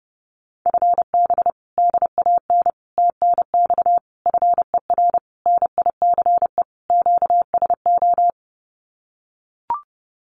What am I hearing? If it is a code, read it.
Morse code, heard as a keyed tone, about 30 wpm, 700 Hz. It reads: FB DAN TNX FER NICE QSO
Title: Common QSO Phrases Encore 30wpm